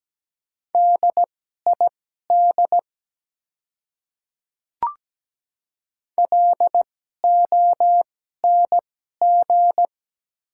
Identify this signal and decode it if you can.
Morse code, heard as a keyed tone, about 17 wpm, spaced as for 12 wpm, 700 Hz. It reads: DID LONG
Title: 2x Character Spacing - Top 200 Words - 17wpm